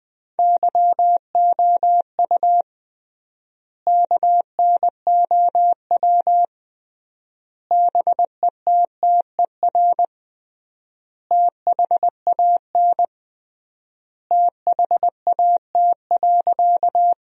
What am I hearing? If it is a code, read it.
Morse code, heard as a keyed tone, about 21 wpm, 700 Hz. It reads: YOU KNOW BETTER THAN THAT.
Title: 3x Word-Spacing - Sentences from Top 400 Words - 20wpm